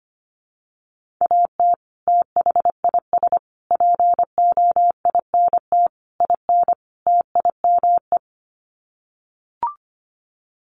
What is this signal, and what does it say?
Morse code, heard as a keyed tone, about 25 wpm, 700 Hz. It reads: AT THIS POINT IN TIME
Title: Sets of 5 Words Encore 25wpm